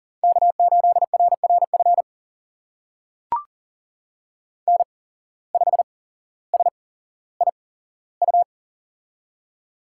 Morse code, heard as a keyed tone, about 39 wpm, spaced as for 17 wpm, 700 Hz. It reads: K8RRF N5SIU
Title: USA Call Signs - 8x Character Spacing - 40wpm